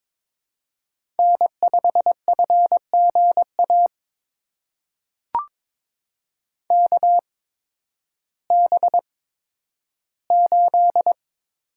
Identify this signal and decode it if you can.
Morse code, heard as a keyed tone, about 22 wpm, spaced as for 12 wpm, 700 Hz. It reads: N5FGA KB8
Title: USA Call Signs - 8x Character Spacing - 22wpm